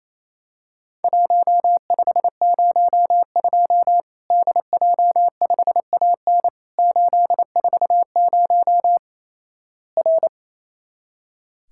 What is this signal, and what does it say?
Morse code, heard as a keyed tone, about 28 wpm, 700 Hz. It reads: 1502 DJ5AN 840 R
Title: Worked All Europe DX Contest - QTC Exchange - 28wpm